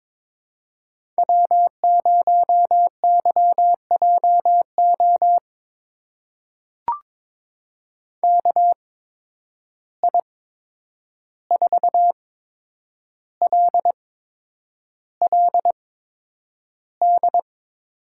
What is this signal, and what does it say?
Morse code, heard as a keyed tone, about 22 wpm, spaced as for 9 wpm, 700 Hz. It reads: W0YJO KI4LLD
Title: USA Call Signs - 8x Character Spacing - 22wpm